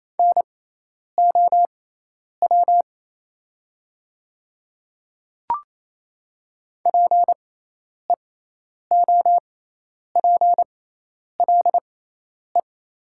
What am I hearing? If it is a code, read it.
Morse code, heard as a keyed tone, about 27 wpm, spaced as for 10 wpm, 700 Hz. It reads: NOW PEOPLE
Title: 6x Character Spacing - Top 300 Words - 28wpm